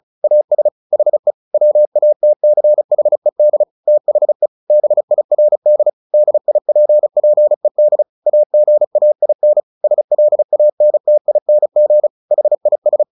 Morse code, heard as a keyed tone, about 34 wpm, 600 Hz. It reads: AS HE WATCHED THE BIRD DIPPED AGAIN SLANTING HIS
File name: the old man and the sea / the old man and the sea 1 35wpm